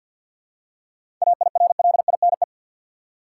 Morse code, heard as a keyed tone, about 53 wpm, 700 Hz. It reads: AIRLINE
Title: Seven Letter Words - 50wpm